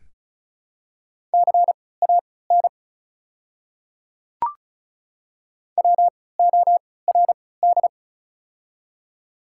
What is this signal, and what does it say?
Morse code, heard as a keyed tone, about 36 wpm, spaced as for 20 wpm, 700 Hz. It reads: CAN WORD
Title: 3x Character Spacing - Top 100 Words - 35wpm